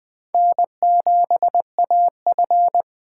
Morse code, heard as a keyed tone, about 20 wpm, 700 Hz. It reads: N7AF
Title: USA Call Signs - 4 Characters - 4x Character Spacing - 20wpm